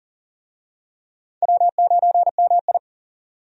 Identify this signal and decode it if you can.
Morse code, heard as a keyed tone, about 40 wpm, 700 Hz. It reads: W9MI
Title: USA Call Signs - 4 Characters - 40wpm